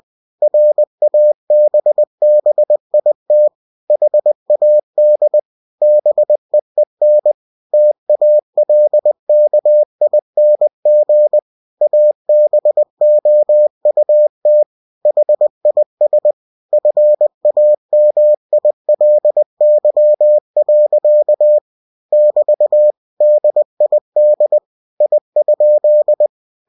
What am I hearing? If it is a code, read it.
Morse code, heard as a keyed tone, about 20 wpm, 600 Hz. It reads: RABBIT HAD BEEN TALKING ABOUT HIS FAMILY. <BT> DID I?
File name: cw-books / winnie-the-pooh / winnie-20wpm-page-0061